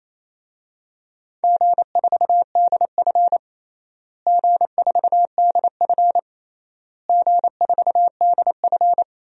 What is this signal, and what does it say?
Morse code, heard as a keyed tone, about 28 wpm, 700 Hz. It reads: G4DF G4DF G4DF
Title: DXCC-100 Call Signs - 4 Characters - Repeated 3x - 28wpm